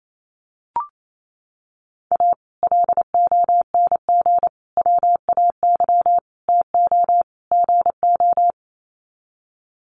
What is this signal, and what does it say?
Morse code, heard as a keyed tone, about 28 wpm, 700 Hz. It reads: A LONG WAY TO GO
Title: Sets of 5 Words 28wpm